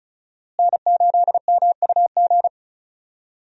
Morse code, heard as a keyed tone, about 34 wpm, 700 Hz. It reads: N8MUG
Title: USA Call Signs - 5 Characters - Repeated 3x - 35wpm